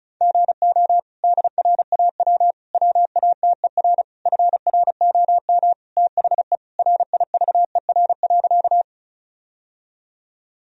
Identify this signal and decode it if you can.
Morse code, heard as a keyed tone, about 36 wpm, 700 Hz. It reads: GO DRAW WATER FROM THE RIVER.
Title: Sentences from Top 1000 Words - Review Part 1 - 35wpm